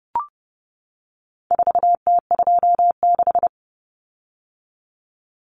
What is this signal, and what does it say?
Morse code, heard as a keyed tone, about 30 wpm, 700 Hz. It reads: 4T26
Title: Common Cut Numbers - 4-Digit 30wpm